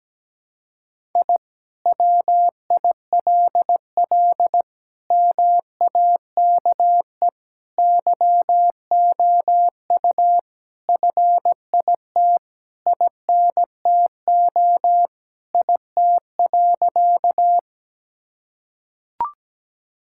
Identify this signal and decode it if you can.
Morse code, heard as a keyed tone, about 17 wpm, 700 Hz. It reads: I WILL MAKE YOU FIT INTO IT.
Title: Sentences from Top 1000 Words - Review Part 5 - 17wpm